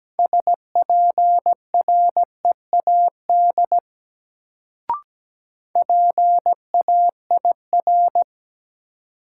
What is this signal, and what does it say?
Morse code, heard as a keyed tone, about 17 wpm, 700 Hz. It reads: SPREAD PAIR
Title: Sentences from Top 1200 Words - New Word Drill - 17wpm